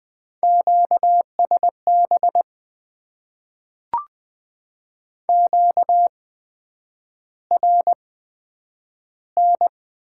Morse code, heard as a keyed tone, about 20 wpm, spaced as for 9 wpm, 700 Hz. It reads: QSB QRN
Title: Top 90 QSO Elements - 8x Character Spacing - 20wpm